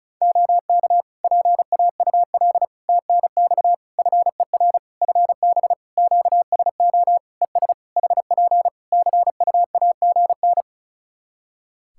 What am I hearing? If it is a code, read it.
Morse code, heard as a keyed tone, about 36 wpm, 700 Hz. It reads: OK PAUL TNX FER FB QSO ES HP CUAGN
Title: K1USN SST QSO Practice 35wpm